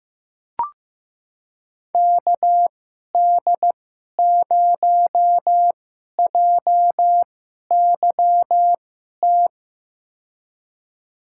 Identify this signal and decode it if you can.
Morse code, heard as a keyed tone, about 15 wpm, spaced as for 11 wpm, 700 Hz. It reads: KD0JYT
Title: USA Call Signs - 6 Characters - 2x Character Spacing - 15wpm